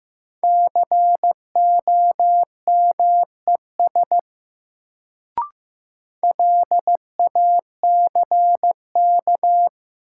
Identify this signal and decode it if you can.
Morse code, heard as a keyed tone, about 15 wpm, 700 Hz. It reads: COMES LACK
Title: Sets of 3 Words - New Word Drill - 15wpm